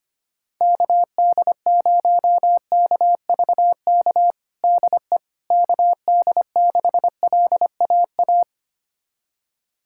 Morse code, heard as a keyed tone, about 25 wpm, 700 Hz. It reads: KD0KVK DE KD6LAA
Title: USA Callsign DE Callsign - 6 Characters - 25wpm